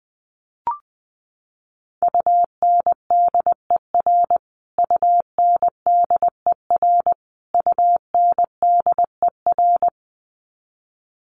Morse code, heard as a keyed tone, about 20 wpm, 700 Hz. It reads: UNDER UNDER UNDER
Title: Top 500 Words Repeated 20wpm